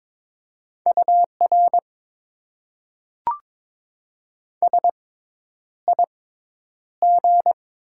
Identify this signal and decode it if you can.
Morse code, heard as a keyed tone, about 22 wpm, spaced as for 10 wpm, 700 Hz. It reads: UR SIG
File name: Top 90 QSO Elements - 6x Character Spacing - 22wpm